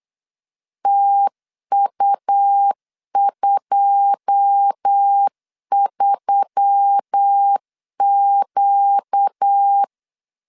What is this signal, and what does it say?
Morse code, heard as a keyed tone, about 8 wpm, 800 Hz. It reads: TU23Q
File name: CW / DET / 1 9 1500C S C 8wpm 5653